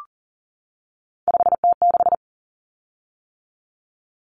Morse code, heard as a keyed tone, about 40 wpm, 700 Hz. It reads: 5T6
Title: Common Cut Numbers - 3-Digit 40wpm